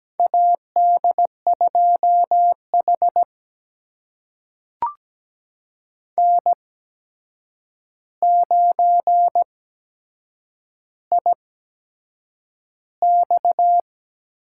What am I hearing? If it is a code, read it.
Morse code, heard as a keyed tone, about 17 wpm, spaced as for 7 wpm, 700 Hz. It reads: AD2H N9IX
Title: USA Call Signs - 4 Characters - 8x Character Spacing - 17wpm